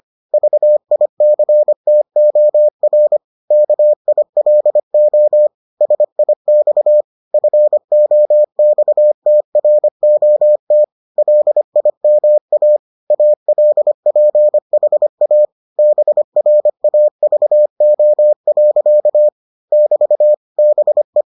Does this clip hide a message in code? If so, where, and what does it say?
Yes: Morse code from 0.3 to 21.2 s, heard as a keyed tone, about 25 wpm, 600 Hz. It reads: VICTOR KILO SIX FOXTROT LIMA ALPHA BRAVO. <BT> BE